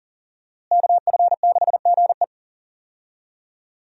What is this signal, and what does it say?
Morse code, heard as a keyed tone, about 40 wpm, 700 Hz. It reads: KF6GE
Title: USA Call Signs - 5 Characters - Repeated 2x - 40wpm